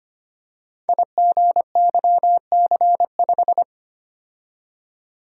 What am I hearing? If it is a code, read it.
Morse code, heard as a keyed tone, about 25 wpm, 700 Hz. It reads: IGYC5
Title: Code Groups - 5 Letter:Numbers - 25wpm